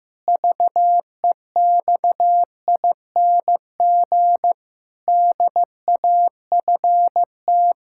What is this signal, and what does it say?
Morse code, heard as a keyed tone, about 15 wpm, 700 Hz. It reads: VEXING DAFT
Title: Pangram Challenge v2 - 15wpm